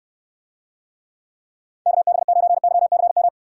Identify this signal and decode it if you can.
Morse code, heard as a keyed tone, about 69 wpm, 700 Hz. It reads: KB9YCM